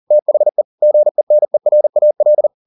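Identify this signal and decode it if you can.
Morse code, heard as a keyed tone, about 40 wpm, 600 Hz. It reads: THE GENERAL
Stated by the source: Morse code at 40 wpm, practice text, CW med Gnister.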